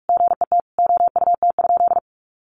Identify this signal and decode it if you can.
Morse code, heard as a keyed tone, about 44 wpm, 700 Hz. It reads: GET OUT?
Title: Sentences from Top 1400 Words 45wpm